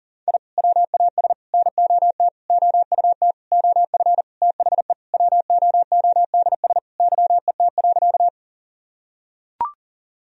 Morse code, heard as a keyed tone, about 40 wpm, 700 Hz. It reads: I WAS NOT OUT OF THE WOODS YET.
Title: Sentences from Top 1000 Words - Review Part 2 - 40wpm